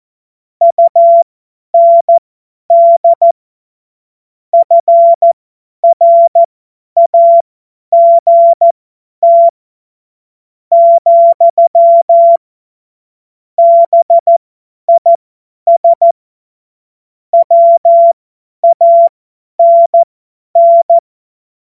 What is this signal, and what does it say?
Morse code, heard as a keyed tone, about 14 wpm, spaced as for 9 wpm, 680 Hz. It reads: UND FRAGT, BIS WANN
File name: Morsekurs / Witze / Witz 14. Juni 2022 50 F70